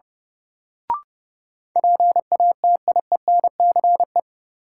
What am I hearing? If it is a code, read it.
Morse code, heard as a keyed tone, about 31 wpm, 700 Hz. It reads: PATIENCE
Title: Sentences from Top 1300 Words - New Word Drill - 30wpm